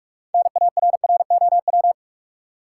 Morse code, heard as a keyed tone, about 42 wpm, 700 Hz. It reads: NARROW
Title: Six Letter Words - 45wpm